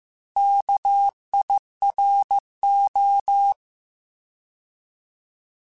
Morse code, heard as a keyed tone, about 15 wpm, 800 Hz. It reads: KIRO
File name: Alphabetical / Alphabetical mp3s / Ks @13